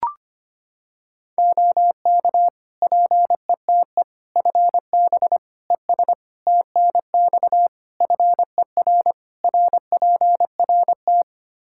Morse code, heard as a keyed tone, about 25 wpm, 700 Hz. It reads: OK PETE FB ES TNX FER RPRT
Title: Practice QSO - Exchange 2 - Info - 25wpm